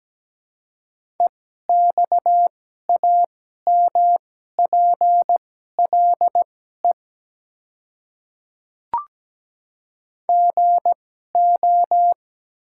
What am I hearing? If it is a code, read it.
Morse code, heard as a keyed tone, about 17 wpm, spaced as for 12 wpm, 700 Hz. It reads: EXAMPLE GO
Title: 2x Character Spacing - Top 300 Words - 17wpm